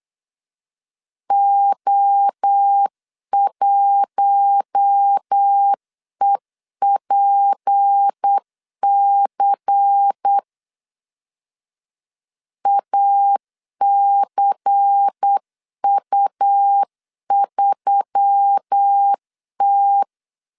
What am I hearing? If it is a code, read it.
Morse code, heard as a keyed tone, about 8 wpm, 800 Hz. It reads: O1EPC ACU3T
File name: CW / DET / 1 6 1500C S C 8wpm 5219